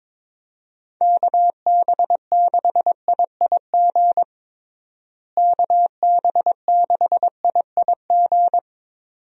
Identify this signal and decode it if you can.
Morse code, heard as a keyed tone, about 22 wpm, 700 Hz. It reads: KB6IIG KB6IIG
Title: USA Call Signs - 6 Characters - Repeated 2x - 22wpm